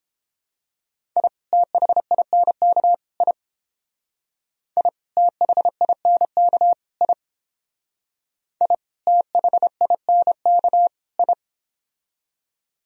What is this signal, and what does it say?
Morse code, heard as a keyed tone, about 30 wpm, 700 Hz. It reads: I THINK I I THINK I I THINK I